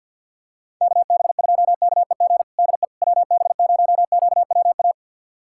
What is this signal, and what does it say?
Morse code, heard as a keyed tone, about 50 wpm, 700 Hz. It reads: KB2KEG DE WB0QWA
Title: USA Callsign DE Callsign 50wpm